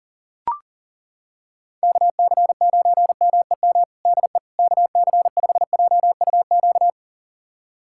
Morse code, heard as a keyed tone, about 40 wpm, 700 Hz. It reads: KC9MEM DE KC5JUQ
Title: USA Callsign DE Callsign - 6 Characters - 40wpm